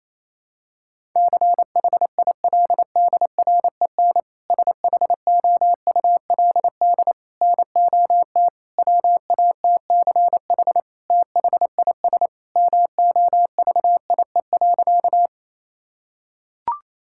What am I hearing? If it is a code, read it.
Morse code, heard as a keyed tone, about 28 wpm, 700 Hz. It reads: CHILDREN SHOULD NOT WATCH THIS MOVIE.
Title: Sentences from Top 1000 Words - Review Part 5 - 28wpm